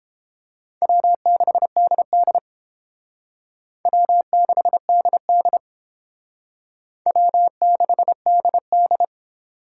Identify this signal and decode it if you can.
Morse code, heard as a keyed tone, about 29 wpm, 700 Hz. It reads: W6DD W6DD W6DD